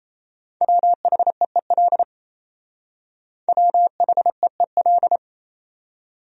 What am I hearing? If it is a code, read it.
Morse code, heard as a keyed tone, about 34 wpm, 700 Hz. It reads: WHEEL WHEEL
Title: Speed-Racing Top 500 Words 28wpm